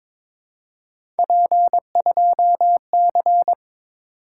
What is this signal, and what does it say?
Morse code, heard as a keyed tone, about 22 wpm, 700 Hz. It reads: P2C
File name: practice / 22wpm / Three Letter-Number Combinations - 22wpm